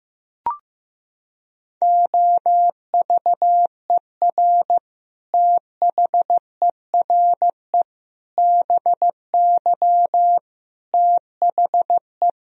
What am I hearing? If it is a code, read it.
Morse code, heard as a keyed tone, about 15 wpm, 700 Hz. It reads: OVER THERE BY THE
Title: Sentences from Top 1000 Words - Review Part 3 - 15wpm